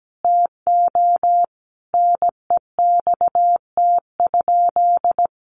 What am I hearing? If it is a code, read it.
Morse code, heard as a keyed tone, about 17 wpm, 700 Hz. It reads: TO NEXT?
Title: Sentences from Top 1200 Words 17wpm